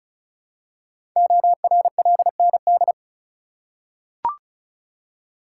Morse code, heard as a keyed tone, about 36 wpm, 700 Hz. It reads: ORLND
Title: Code Groups - 5 Letters - 35wpm